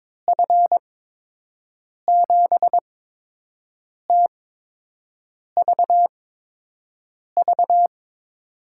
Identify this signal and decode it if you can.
Morse code, heard as a keyed tone, about 22 wpm, spaced as for 6 wpm, 700 Hz. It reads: F7TVV
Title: USA Call Signs - 6 Characters - 8x Character Spacing - 22wpm